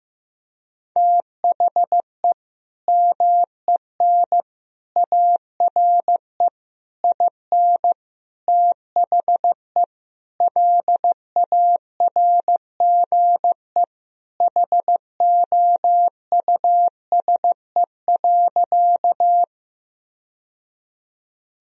Morse code, heard as a keyed tone, about 15 wpm, 700 Hz. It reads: THE MEN ARE IN THE LARGE HOUSE.
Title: Sentences from Top 1000 Words - Review Part 1 - 15wpm